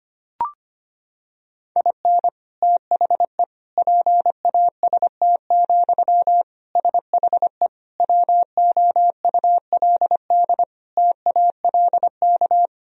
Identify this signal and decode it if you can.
Morse code, heard as a keyed tone, about 25 wpm, 700 Hz. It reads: IN THE PAST, SHE WOULD TALK
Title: Sentences from Top 1000 Words - Review Part 3 - 25wpm